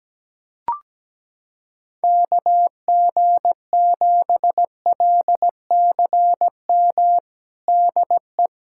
Morse code, heard as a keyed tone, about 17 wpm, 700 Hz. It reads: KG7LCM DE
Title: USA Callsign DE Callsign - 6 Characters - 17wpm